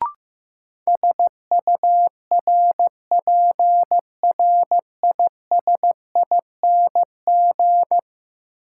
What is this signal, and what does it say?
Morse code, heard as a keyed tone, about 15 wpm, 700 Hz. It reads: SURPRISING
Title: Sets of 5 Words - New Word Drill - 15wpm